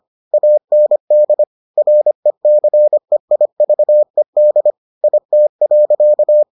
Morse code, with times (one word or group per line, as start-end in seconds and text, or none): 0.3-1.4 AND
1.8-4.7 RECEIVED
5.0-6.4 IT.